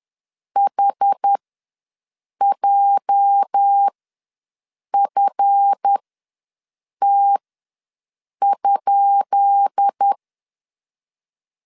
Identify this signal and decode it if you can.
Morse code, heard as a keyed tone, about 10 wpm, spaced as for 6 wpm, 800 Hz. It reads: HJFT?
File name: LEZIONI / 10 SIGNS / 10B INT ws 10wpm 4134